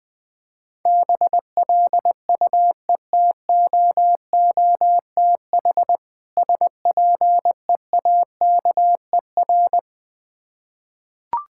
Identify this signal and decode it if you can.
Morse code, heard as a keyed tone, about 20 wpm, 700 Hz. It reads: BLUETOOTH SPEAKER
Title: Difficult Words with Hints - 20wpm